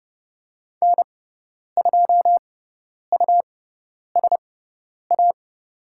Morse code, heard as a keyed tone, about 30 wpm, spaced as for 10 wpm, 700 Hz. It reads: N2USA
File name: Lesson 18g - Letter m - Call Signs 30wpm